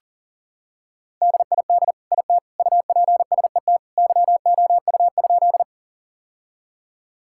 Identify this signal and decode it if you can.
Morse code, heard as a keyed tone, about 40 wpm, 700 Hz. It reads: DID IT UPSET YOU?
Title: Sentences from Top 1000 Words - Review Part 5 - 40wpm